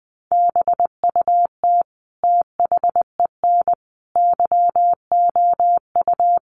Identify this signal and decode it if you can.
Morse code, heard as a keyed tone, about 20 wpm, 700 Hz. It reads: BUT THEN YOU
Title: Sentences from Top 300 Words 20wpm